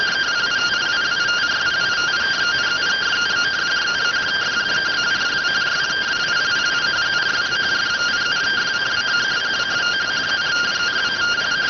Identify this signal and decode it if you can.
Two-tone teleprinter signal, about 170 Hz shift.